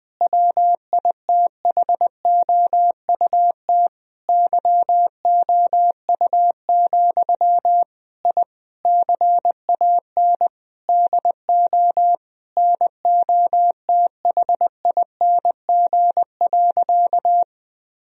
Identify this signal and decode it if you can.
Morse code, heard as a keyed tone, about 19 wpm, 700 Hz. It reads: WITHOUT YOU, I CAN DO NOTHING.
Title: Sentences from Top 1000 Words - Review Part 3 - 20wpm